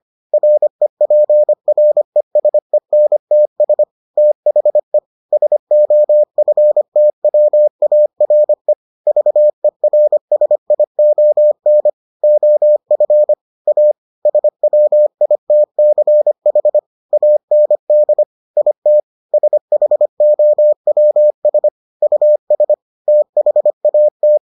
Morse code, heard as a keyed tone, about 25 wpm, 600 Hz. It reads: REPRESENTS THE SOFTWARE VERSION OF A SWITCH AND IT SHOWS US THAT